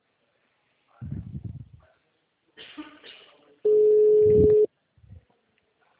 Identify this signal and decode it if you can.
Ringback tone (Europe, 425 Hz), with speech.